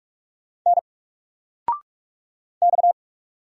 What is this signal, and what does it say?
Morse code, heard as a keyed tone, about 44 wpm, 700 Hz. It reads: N X
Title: Single Character (Letter:Number:Punctuation:Prosign) 45wpm